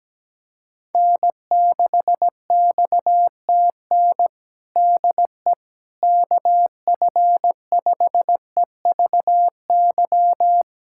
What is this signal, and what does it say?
Morse code, heard as a keyed tone, about 17 wpm, 700 Hz. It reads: N6XTN DE KF5EVY
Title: USA Callsign DE Callsign 17wpm